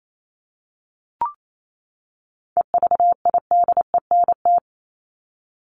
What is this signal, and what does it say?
Morse code, heard as a keyed tone, about 28 wpm, 700 Hz. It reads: EVIDENT
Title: LY Suffix - Top 100 Words with Base Words 28wpm